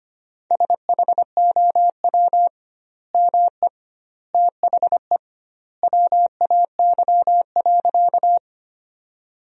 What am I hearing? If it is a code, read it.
Morse code, heard as a keyed tone, about 25 wpm, 700 Hz. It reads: SHOW ME THE WAY.
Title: 2x Word-Spacing - Sentences from Top 200 Words - 25wpm